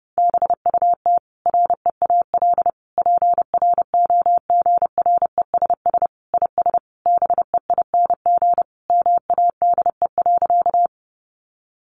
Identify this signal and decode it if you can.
Morse code, heard as a keyed tone, about 30 wpm, 700 Hz. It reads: BUT REAL PROGRESS IS BEING MADE.
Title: Sentences from Top 1400 Words 30wpm